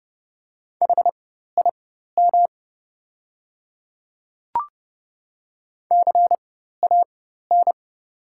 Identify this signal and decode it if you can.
Morse code, heard as a keyed tone, about 30 wpm, spaced as for 14 wpm, 700 Hz. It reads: HIM CAN
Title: 4x Character Spacing - Top 100 Words - 30wpm